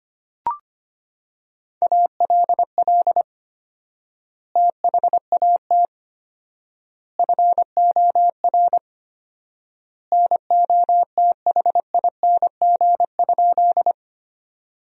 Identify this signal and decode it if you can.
Morse code, heard as a keyed tone, about 25 wpm, 700 Hz. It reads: ALL THAT FOR NOTHING?
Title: 4x Word-Spacing - Sentences from Top 500 Words - 25wpm